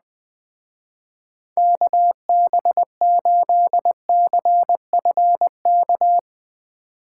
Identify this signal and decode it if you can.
Morse code, heard as a keyed tone, about 21 wpm, 700 Hz. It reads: KB8CFK